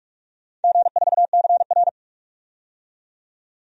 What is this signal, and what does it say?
Morse code, heard as a keyed tone, about 42 wpm, 700 Hz. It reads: G4CR
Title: DXCC-100 Call Signs - 4 Characters - 45wpm